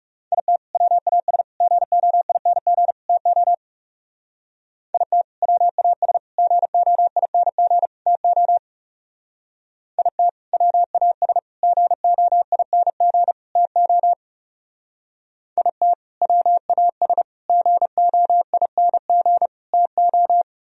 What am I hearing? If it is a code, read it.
Morse code, heard as a keyed tone, about 40 wpm, 700 Hz. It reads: IT WAS GOING TO IT WAS GOING TO IT WAS GOING TO IT WAS GOING TO